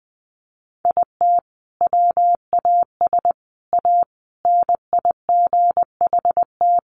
Morse code, heard as a keyed tone, about 20 wpm, 700 Hz. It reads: IT WAS A NIGHT
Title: Sentences from Top 1500 Words 20wpm